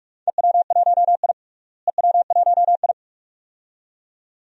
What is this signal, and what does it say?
Morse code, heard as a keyed tone, about 46 wpm, 700 Hz. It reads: EW1I EW1I
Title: DXCC-100 Call Signs - 4 Characters - Repeated 2x - 45wpm